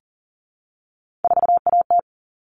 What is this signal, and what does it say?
Morse code, heard as a keyed tone, about 39 wpm, 700 Hz. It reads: 4AT
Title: All Cut Numbers - 3-Digit 40wpm